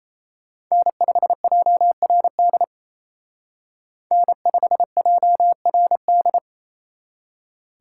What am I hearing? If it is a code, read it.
Morse code, heard as a keyed tone, about 33 wpm, 700 Hz. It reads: N5JRD N5JRD
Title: Speed Racing - USA 1x3 Call Signs 28wpm